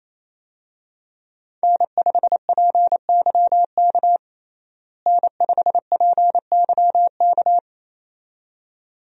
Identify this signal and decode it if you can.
Morse code, heard as a keyed tone, about 28 wpm, 700 Hz. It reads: N5PYK N5PYK
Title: USA Call Signs - 5 Characters - Repeated 2x - 28wpm